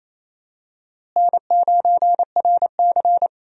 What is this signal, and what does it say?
Morse code, heard as a keyed tone, about 28 wpm, 700 Hz. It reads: N9RC
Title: USA Call Signs - 4 Characters - Repeated 3x - 28wpm